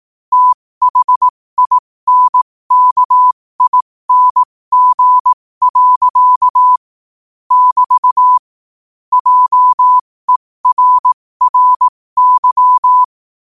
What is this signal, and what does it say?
Morse code, heard as a keyed tone, about 18 wpm, spaced as for 15 wpm, 1000 Hz. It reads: THINKING. <BT> JERRY
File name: qotdfiles / 15wpm